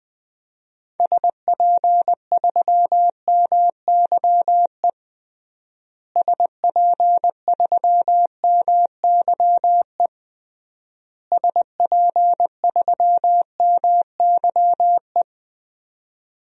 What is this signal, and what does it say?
Morse code, heard as a keyed tone, about 21 wpm, 700 Hz. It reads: SP3MYE SP3MYE SP3MYE
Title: DXCC-100 Call Signs - 6 Characters - Repeated 3x - 20wpm